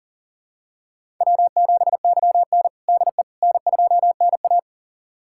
Morse code, heard as a keyed tone, about 40 wpm, 700 Hz. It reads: W7YN DE N2NA
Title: USA Callsign DE Callsign - 4 Characters - 40wpm